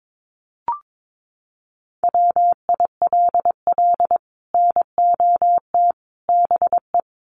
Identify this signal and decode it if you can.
Morse code, heard as a keyed tone, about 22 wpm, 700 Hz. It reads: WILL NOT BE
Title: Sets of 5 Words 22wpm